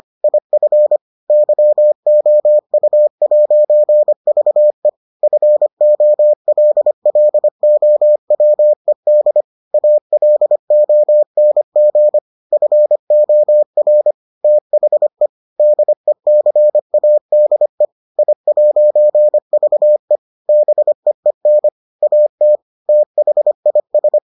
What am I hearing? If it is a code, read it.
Morse code, heard as a keyed tone, about 25 wpm, 600 Hz. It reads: IF YOU'VE FOLLOWED ALONG FOR THE DECADE I'VE BEEN AT THIS